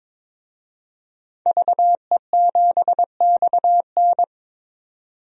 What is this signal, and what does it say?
Morse code, heard as a keyed tone, about 22 wpm, 700 Hz. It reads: VE7XN
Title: DXCC-100 Call Signs - 5 Characters - 22wpm